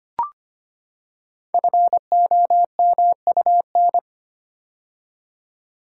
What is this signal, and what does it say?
Morse code, heard as a keyed tone, about 25 wpm, 700 Hz. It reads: FOMUN
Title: Code Groups - 5 Letters - 25wpm